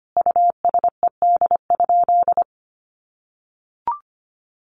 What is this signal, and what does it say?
Morse code, heard as a keyed tone, about 25 wpm, 700 Hz. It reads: USED?
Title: Sentences from Top 1200 Words 25wpm